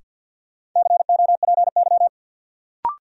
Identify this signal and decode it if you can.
Morse code, heard as a keyed tone, about 52 wpm, 700 Hz. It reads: COPY
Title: Sentences from Top 1400 Words - New Word Drill Alt - 50wpm